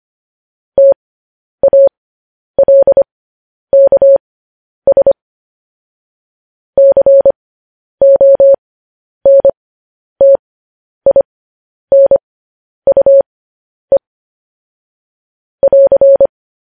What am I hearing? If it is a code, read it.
Morse code, heard as a keyed tone, about 25 wpm, spaced as for 10 wpm, 550 Hz. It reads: TALKS CONTINUE <AR>
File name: podcast / News headlines-1x-25x10